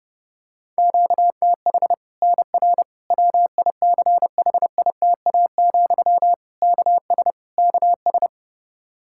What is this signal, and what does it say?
Morse code, heard as a keyed tone, about 31 wpm, 700 Hz. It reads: QTH NR WICHITA, KS KS
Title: Practice QSO - Exchange 1 - Beginning - 30wpm